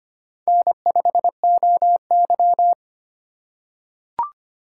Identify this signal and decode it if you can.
Morse code, heard as a keyed tone, about 25 wpm, 700 Hz. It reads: N5OY
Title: USA Call Signs - 3x Character Spacing - 25wpm